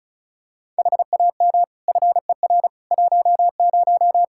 Morse code, heard as a keyed tone, about 36 wpm, 700 Hz. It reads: HAM FER 10
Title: K1USN SST QSO Practice 35wpm